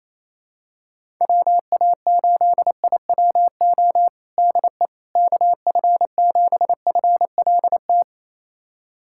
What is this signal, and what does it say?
Morse code, heard as a keyed tone, about 27 wpm, 700 Hz. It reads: WA8IWO DE KF7FLT
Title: USA Callsign DE Callsign 28wpm